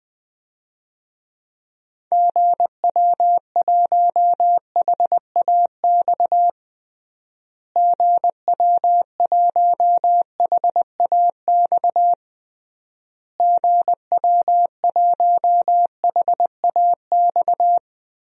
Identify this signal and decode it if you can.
Morse code, heard as a keyed tone, about 20 wpm, 700 Hz. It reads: GW1HAX GW1HAX GW1HAX
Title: DXCC-100 Call Signs - 6 Characters - Repeated 3x - 20wpm